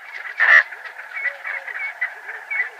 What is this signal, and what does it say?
There is no code.